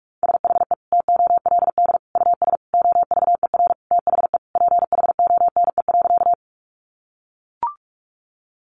Morse code, heard as a keyed tone, about 44 wpm, 700 Hz. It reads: SHE TOLD US OVER THE PHONE.
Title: Sentences from Top 1100 Words 45wpm